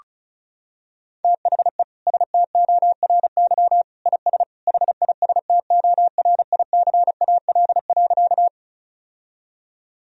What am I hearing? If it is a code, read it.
Morse code, heard as a keyed tone, about 34 wpm, 700 Hz. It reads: THE STORY IS HISTORICAL.
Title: Sentences from Top 1000 Words - Review Part 4 - 35wpm